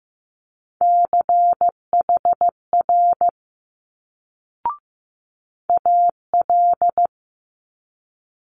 Morse code, heard as a keyed tone, about 15 wpm, 700 Hz. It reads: CHR AL
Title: Multi-letter Phonemes Quick Fire 15wpm